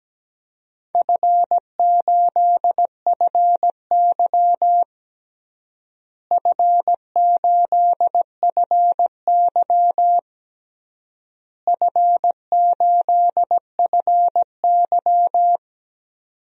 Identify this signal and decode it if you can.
Morse code, heard as a keyed tone, about 17 wpm, 700 Hz. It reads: F8FY F8FY F8FY
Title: DXCC-100 Call Signs - 4 Characters - Repeated 3x - 17wpm